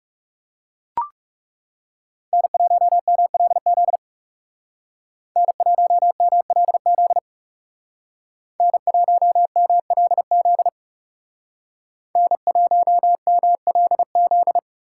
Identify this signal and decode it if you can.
Morse code, heard as a keyed tone, about 37 wpm, 700 Hz. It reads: N1MLZ N1MLZ N1MLZ N1MLZ
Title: Speed Racing - USA Call Signs 30wpm